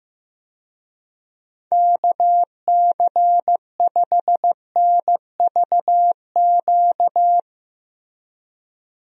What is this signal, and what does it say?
Morse code, heard as a keyed tone, about 16 wpm, 700 Hz. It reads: KC5NVQ
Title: USA Call Signs - 6 Characters - Repeated 2x - 15wpm